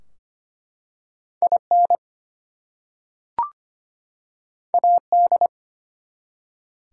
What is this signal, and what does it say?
Morse code, heard as a keyed tone, about 25 wpm, 700 Hz. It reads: IN AD
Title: Two Letter Words - 25wpm